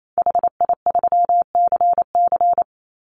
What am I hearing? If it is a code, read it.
Morse code, heard as a keyed tone, about 28 wpm, 700 Hz. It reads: HI3CC
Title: Call Signs - 28wpm